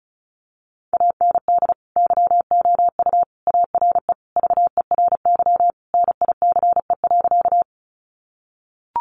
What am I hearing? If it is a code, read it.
Morse code, heard as a keyed tone, about 34 wpm, 700 Hz. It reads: AND YOU ARE VERY NICE.
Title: Sentences from Top 1100 Words 35wpm